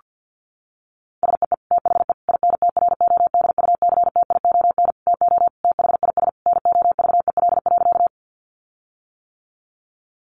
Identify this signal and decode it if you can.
Morse code, heard as a keyed tone, about 50 wpm, 700 Hz. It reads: SEE THE INTRODUCTION TO THIS NOVEL.
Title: Sentences from Top 1300 Words 50wpm